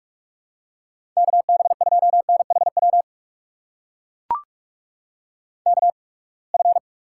Morse code, heard as a keyed tone, about 44 wpm, spaced as for 31 wpm, 700 Hz. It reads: KB2NHW KF
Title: USA Call Signs - 6 Characters - 8x Character Spacing - 45wpm